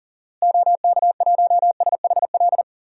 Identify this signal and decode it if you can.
Morse code, heard as a keyed tone, about 38 wpm, 700 Hz. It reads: OK1SHL
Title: DXCC-100 Call Signs - 6 Characters - Repeated 3x - 40wpm